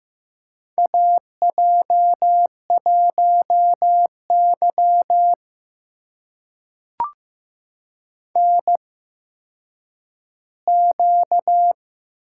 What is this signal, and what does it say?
Morse code, heard as a keyed tone, about 15 wpm, spaced as for 9 wpm, 700 Hz. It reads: AJ1Y NQ
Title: USA Call Signs - 4 Characters - 8x Character Spacing - 15wpm